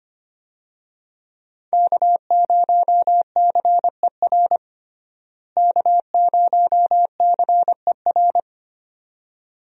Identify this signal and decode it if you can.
Morse code, heard as a keyed tone, about 25 wpm, 700 Hz. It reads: K0CER K0CER
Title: USA Call Signs - 5 Characters - Repeated 2x - 25wpm